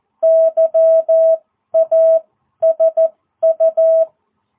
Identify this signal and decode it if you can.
Morse code, heard as a keyed tone, about 14 wpm, spaced as for 11 wpm, 660 Hz. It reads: YASU